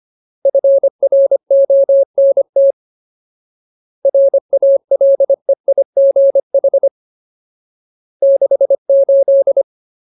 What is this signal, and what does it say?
Morse code, heard as a keyed tone, about 25 wpm, 550 Hz. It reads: FRONT RALEIGH 68
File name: Ultimate Word List / 25WPM Ultimate